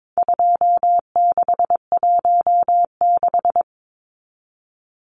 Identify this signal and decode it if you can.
Morse code, heard as a keyed tone, about 22 wpm, 700 Hz. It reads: 2616
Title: Numbers - 4-Digit 22wpm